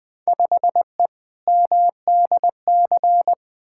Morse code, heard as a keyed tone, about 20 wpm, 700 Hz. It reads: 5E MDC
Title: Field Day Exchanges - 20wpm